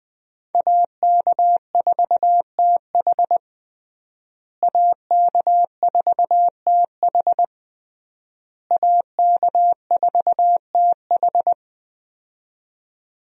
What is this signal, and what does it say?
Morse code, heard as a keyed tone, about 21 wpm, 700 Hz. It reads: AK4TH AK4TH AK4TH
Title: USA Call Signs - 5 Characters - Repeated 3x - 20wpm